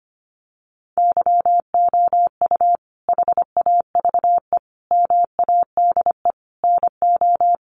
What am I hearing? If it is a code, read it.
Morse code, heard as a keyed tone, about 25 wpm, 700 Hz. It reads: YOU HAVE MADE NO
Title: Sentences from Top 1400 Words 25wpm